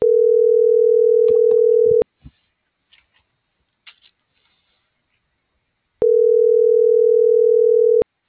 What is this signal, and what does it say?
Ringback tone (North America), 440 + 480 Hz.